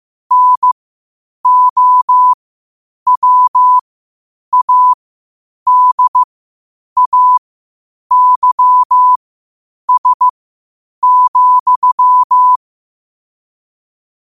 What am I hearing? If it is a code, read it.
Morse code, heard as a keyed tone, about 15 wpm, spaced as for 8 wpm, 1000 Hz. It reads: NOWADAYS,
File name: 8wpm